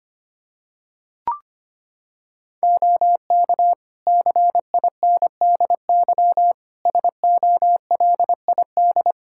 Morse code, heard as a keyed tone, about 25 wpm, 700 Hz. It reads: OK CINDY SOLID
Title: K1USN SST QSO Practice 25wpm